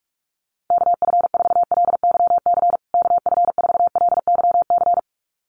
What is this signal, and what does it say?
Morse code, heard as a keyed tone, about 44 wpm, 700 Hz. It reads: KF4LYC KF4LYC
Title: USA Call Signs Repeated 2x 45wpm